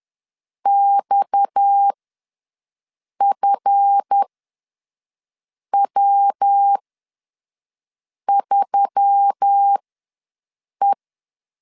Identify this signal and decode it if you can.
Morse code, heard as a keyed tone, about 10 wpm, spaced as for 5 wpm, 800 Hz. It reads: XFW3E
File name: LEZIONI / 9 286 ws 10wpm 4032